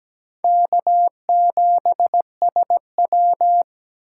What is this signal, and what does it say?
Morse code, heard as a keyed tone, about 17 wpm, 700 Hz. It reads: K7SW
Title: USA Call Signs - 4 Characters - 8x Character Spacing - 17wpm